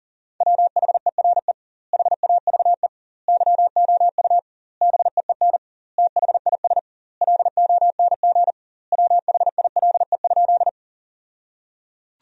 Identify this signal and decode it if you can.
Morse code, heard as a keyed tone, about 37 wpm, 700 Hz. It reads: WHERE HAVE YOU BEEN THIS LONG WHILE?
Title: 2x Word-Spacing - Sentences from Top 300 Words - 40wpm